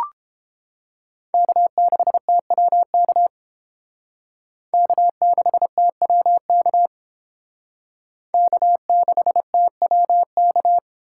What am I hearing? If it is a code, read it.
Morse code, heard as a keyed tone, about 31 wpm, 700 Hz. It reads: K6TWK K6TWK K6TWK